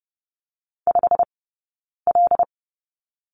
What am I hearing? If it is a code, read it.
Morse code, heard as a keyed tone, about 30 wpm, spaced as for 9 wpm, 690 Hz. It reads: 5L
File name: course / 30:10wpm / 086 Lesson 15d - Number 5 - Two Characters 30wpm